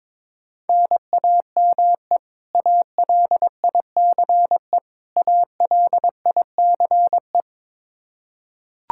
Morse code, heard as a keyed tone, about 21 wpm, 700 Hz. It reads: NAME ALICE ALICE
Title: Practice QSO - Exchange 1 - Beginning - 22wpm